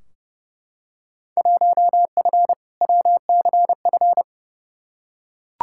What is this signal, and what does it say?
Morse code, heard as a keyed tone, about 30 wpm, 700 Hz. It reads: 1F WCF
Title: Field Day Exchanges - 30wpm